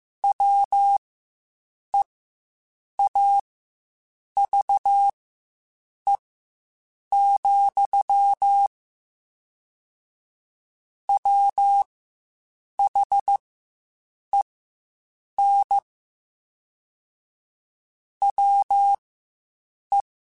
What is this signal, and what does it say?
Morse code, heard as a keyed tone, about 15 wpm, spaced as for 7 wpm, 800 Hz. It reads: WEAVE, WHEN WE
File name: Quotes Spaced / Quotes Spaced-mp3s / Deceive sp @10